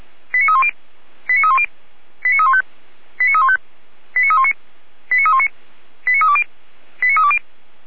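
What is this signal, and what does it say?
Five-tone selective call (ZVEI), heard as quick runs of five tones, tones of about 70 ms. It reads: 89310 89310 89315 89315 89319 89319 89320 89320
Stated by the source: ZVEI five-tone selective call received by SDR, from feuerwehr-orlamuende.org (Orlamünde fire brigade).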